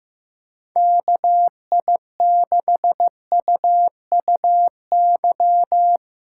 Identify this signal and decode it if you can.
Morse code, heard as a keyed tone, about 15 wpm, 700 Hz. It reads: KI6UUY
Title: USA Call Signs - 8x Character Spacing - 15wpm